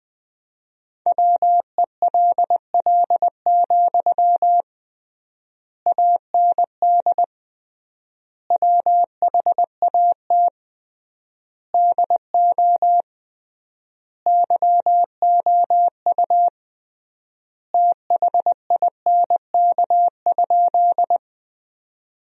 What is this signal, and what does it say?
Morse code, heard as a keyed tone, about 20 wpm, 700 Hz. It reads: WELL, AND WHAT DO YOU THINK?
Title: 3x Word-Spacing - Sentences from Top 200 Words - 20wpm